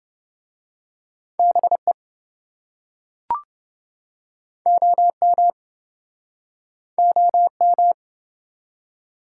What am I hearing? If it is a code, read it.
Morse code, heard as a keyed tone, about 26 wpm, 700 Hz. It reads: BE OM OM